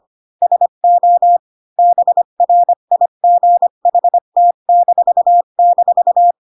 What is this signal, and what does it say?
Morse code, heard as a keyed tone, about 25 wpm, 700 Hz. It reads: SO BRIGHT--
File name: Pinocchio / Pinocchio ch7 20 25 ik1hge